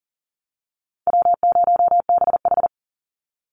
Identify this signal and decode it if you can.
Morse code, heard as a keyed tone, about 40 wpm, 700 Hz. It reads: W0BH
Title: Easy Call Signs - 40wpm